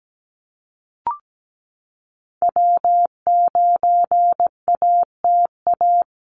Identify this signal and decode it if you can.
Morse code, heard as a keyed tone, about 17 wpm, 700 Hz. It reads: W9ATA
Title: USA Call Signs Repeated 3x 17wpm